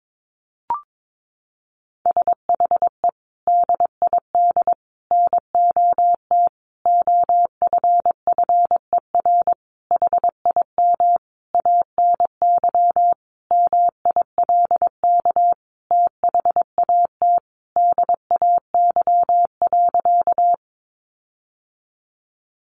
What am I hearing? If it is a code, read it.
Morse code, heard as a keyed tone, about 22 wpm, 700 Hz. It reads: SHE DID NOT OFFER HIM ANY MILK THAT DAY.
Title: Sentences from Top 1500 Words 22wpm